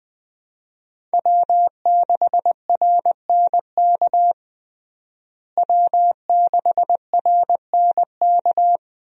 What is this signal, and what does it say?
Morse code, heard as a keyed tone, about 20 wpm, 700 Hz. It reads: W6RNK W6RNK
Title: USA Call Signs - 5 Characters - Repeated 2x - 20wpm